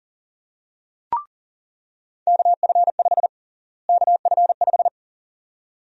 Morse code, heard as a keyed tone, about 40 wpm, 700 Hz. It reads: KF5 KF5
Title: USA Call Sign Prefixes - Repeated 2x - 40wpm